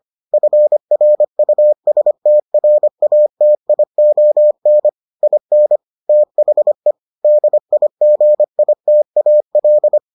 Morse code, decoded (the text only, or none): FRUSTRATION IN THE DIGITAL